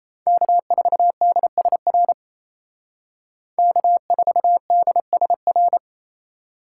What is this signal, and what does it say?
Morse code, heard as a keyed tone, about 35 wpm, 700 Hz. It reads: K4DSR K4DSR